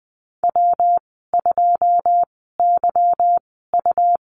Morse code, heard as a keyed tone, about 20 wpm, spaced as for 15 wpm, 700 Hz. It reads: W2YU
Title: 167 Lesson 27g - Number 7 - Call Signs 20wpm